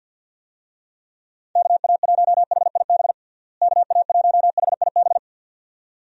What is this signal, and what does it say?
Morse code, heard as a keyed tone, about 46 wpm, 700 Hz. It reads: KA1HIB KA1HIB
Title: USA Call Signs - 6 Characters - Repeated 2x - 50wpm